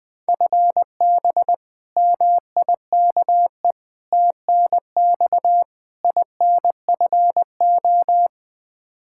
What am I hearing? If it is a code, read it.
Morse code, heard as a keyed tone, about 20 wpm, 700 Hz. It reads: FB MIKE TNX INFO
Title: Practice QSO - Exchange 3 - More Info - 20wpm